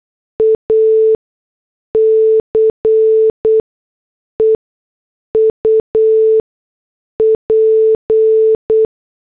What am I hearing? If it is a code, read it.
Morse code, heard as a keyed tone, about 8 wpm, spaced as for 6 wpm, 440 Hz. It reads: ACE UP